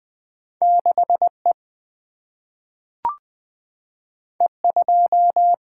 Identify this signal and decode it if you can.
Morse code, heard as a keyed tone, about 20 wpm, 700 Hz. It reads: 6 E E 2
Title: Single-Letter Single-Number 20wpm